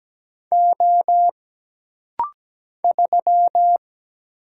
Morse code, heard as a keyed tone, about 17 wpm, 700 Hz. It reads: O 3
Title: Single Letter-Number Alt - Rapid-Fire 17wpm